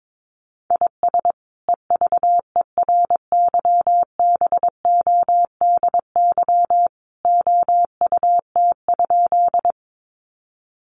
Morse code, heard as a keyed tone, about 22 wpm, 700 Hz. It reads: IS EVERYBODY OUT?
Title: Sentences from Top 1400 Words 22wpm